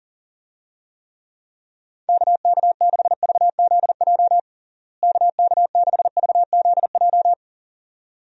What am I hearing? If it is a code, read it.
Morse code, heard as a keyed tone, about 37 wpm, 700 Hz. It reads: KK6VZJ KK6VZJ
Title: USA Call Signs - 6 Characters - Repeated 2x - 40wpm